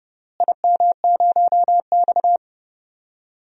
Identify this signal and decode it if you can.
Morse code, heard as a keyed tone, about 30 wpm, 700 Hz. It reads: IM0X
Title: DXCC-100 Call Signs - 4 Characters - Repeated 2x - 30wpm